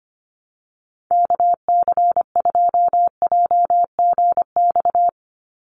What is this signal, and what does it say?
Morse code, heard as a keyed tone, about 25 wpm, 700 Hz. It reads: KC2JGX
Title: USA Call Signs 25wpm